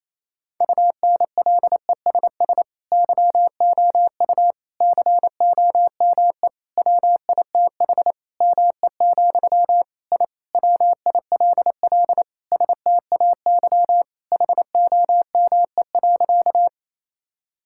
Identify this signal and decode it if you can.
Morse code, heard as a keyed tone, about 28 wpm, 700 Hz. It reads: UNLESS YOU COME WITH ME, I WILL STAY HOME.
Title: Sentences from Top 1000 Words - Review Part 6 - 28wpm